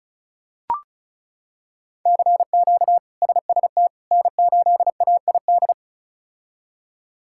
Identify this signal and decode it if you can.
Morse code, heard as a keyed tone, about 34 wpm, 700 Hz. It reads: CQ SST N8AID
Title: K1USN SST QSO Practice - 35wpm